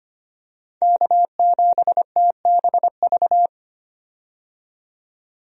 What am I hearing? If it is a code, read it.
Morse code, heard as a keyed tone, about 25 wpm, 700 Hz. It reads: K7TBV
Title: USA Call Signs - 5 Characters - 25wpm